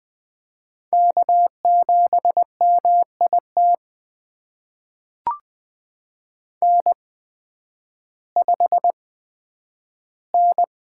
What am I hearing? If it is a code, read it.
Morse code, heard as a keyed tone, about 20 wpm, spaced as for 11 wpm, 700 Hz. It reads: K7MIT N5N
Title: USA Call Signs - 5 Characters - 8x Character Spacing - 20wpm